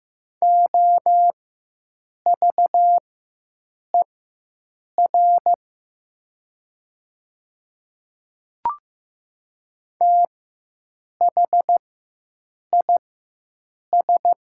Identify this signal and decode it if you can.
Morse code, heard as a keyed tone, about 15 wpm, spaced as for 7 wpm, 700 Hz. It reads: OVER THIS
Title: 4x Character Spacing - Top 200 Words - 15wpm